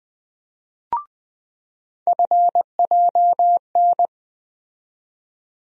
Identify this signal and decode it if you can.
Morse code, heard as a keyed tone, about 20 wpm, 700 Hz. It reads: FJN
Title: USA Call Sign Postfixes - 3 Letters 20wpm